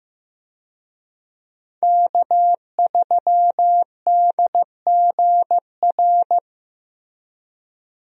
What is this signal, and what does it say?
Morse code, heard as a keyed tone, about 15 wpm, 700 Hz. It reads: K3DGR
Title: USA Call Signs - 5 Characters - Repeated 3x - 15wpm